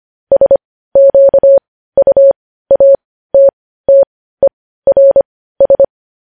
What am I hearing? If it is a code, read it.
Morse code, heard as a keyed tone, about 25 wpm, spaced as for 15 wpm, 550 Hz. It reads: SQUATTERS
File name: podcast / News headlines-1x-25x15